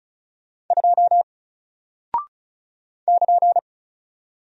Morse code, heard as a keyed tone, about 34 wpm, 700 Hz. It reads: 2 <KN>
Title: Single Character (Letter:Number:Punctuation:Prosign) Alt 35wpm